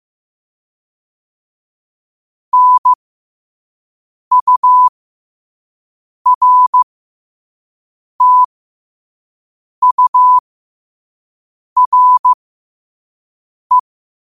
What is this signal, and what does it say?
Morse code, heard as a keyed tone, about 15 wpm, spaced as for 5 wpm, 1000 Hz. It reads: NURTURE
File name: qotdfiles / 5wpm